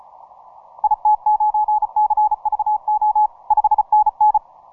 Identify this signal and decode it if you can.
Morse code, heard as a keyed tone, about 34 wpm, 840 Hz. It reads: IT9CVO 5NN
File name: simple-file-list / tx5k 15mcw